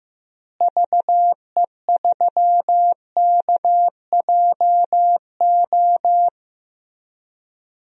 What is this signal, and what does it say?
Morse code, heard as a keyed tone, about 15 wpm, 700 Hz. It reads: VE3KJO
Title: DXCC-100 Call Signs - 6 Characters - Repeated 2x - 15wpm